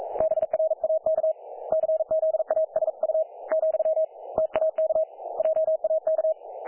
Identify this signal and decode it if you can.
Morse code, heard as a keyed tone, about 40 wpm, 630 Hz. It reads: BRAK CZASU , TAK JAK
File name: dlazaawansowanych / nagraniazpasma / websdr recording 3564.7kHz